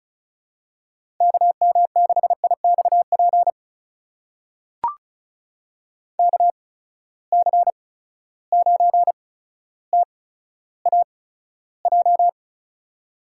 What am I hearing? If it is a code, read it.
Morse code, heard as a keyed tone, about 36 wpm, spaced as for 15 wpm, 700 Hz. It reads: KM6IXP KC9TAJ
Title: USA Call Signs - 6 Characters - 8x Character Spacing - 35wpm